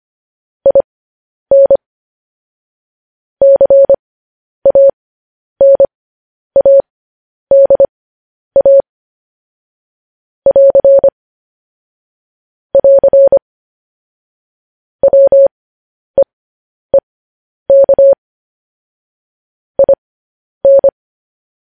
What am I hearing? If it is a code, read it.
Morse code, heard as a keyed tone, about 25 wpm, spaced as for 10 wpm, 550 Hz. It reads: IN CANADA <AR> <AR> WEEK IN